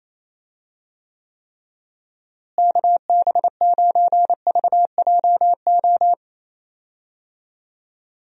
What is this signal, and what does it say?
Morse code, heard as a keyed tone, about 27 wpm, 700 Hz. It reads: KB9VJO
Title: USA Call Signs - 6 Characters - Repeated 3x - 28wpm